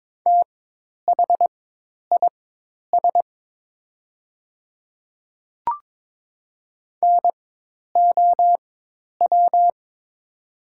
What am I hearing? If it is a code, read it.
Morse code, heard as a keyed tone, about 22 wpm, spaced as for 10 wpm, 690 Hz. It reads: THIS NOW
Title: 4x Character Spacing - Top 200 Words - 22wpm